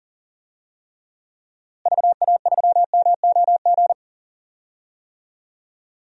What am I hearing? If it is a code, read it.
Morse code, heard as a keyed tone, about 37 wpm, 700 Hz. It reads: VA3MOG
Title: DXCC-100 Call Signs - 6 Characters - Repeated 2x - 40wpm